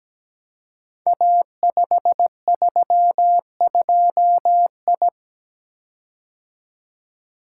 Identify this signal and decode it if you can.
Morse code, heard as a keyed tone, about 17 wpm, 700 Hz. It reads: A532I
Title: Code Groups - 5 Letter:Numbers - 17wpm